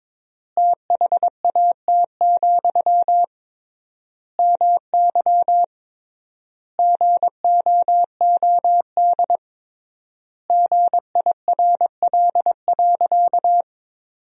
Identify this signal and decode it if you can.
Morse code, heard as a keyed tone, about 21 wpm, 700 Hz. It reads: THAT, MY GOOD GIRL.
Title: 3x Word-Spacing - Sentences from Top 400 Words - 22wpm